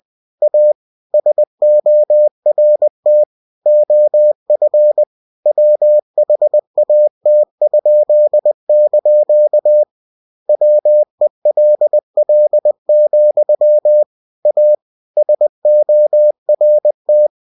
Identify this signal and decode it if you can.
Morse code, heard as a keyed tone, about 20 wpm, 600 Hz. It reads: A SORT OF WHAT?) WELL, A SORT
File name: cw-books / winnie-the-pooh / winnie-20wpm-page-0104